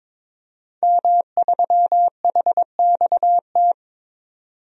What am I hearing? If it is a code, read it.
Morse code, heard as a keyed tone, about 22 wpm, 700 Hz. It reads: M3HXT
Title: DXCC-100 Call Signs - 5 Characters - 22wpm